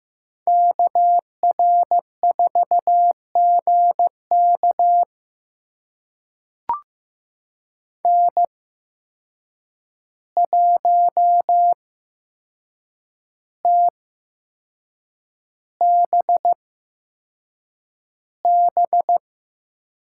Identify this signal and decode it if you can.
Morse code, heard as a keyed tone, about 15 wpm, spaced as for 6 wpm, 700 Hz. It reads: KR4GK N1TBB
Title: USA Call Signs - 5 Characters - 8x Character Spacing - 15wpm